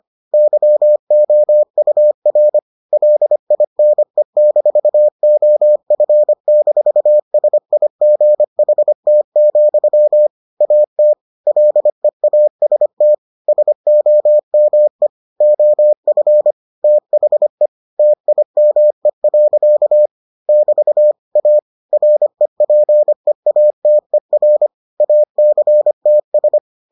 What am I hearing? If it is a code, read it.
Morse code, heard as a keyed tone, about 25 wpm, 600 Hz. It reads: YOUR LINE-OF-SIGHT, AT LEAST SOME OF THE TIME. <BT> A REPEATER ACTS